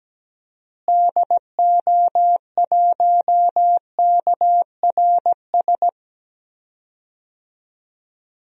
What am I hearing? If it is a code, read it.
Morse code, heard as a keyed tone, about 17 wpm, 700 Hz. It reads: DO1KRS
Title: DXCC-100 Call Signs - 6 Characters - Repeated 3x - 17wpm